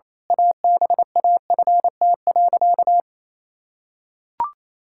Morse code, heard as a keyed tone, about 28 wpm, 700 Hz. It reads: ABAFT.
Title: Pangram Challenge v2 - 28wpm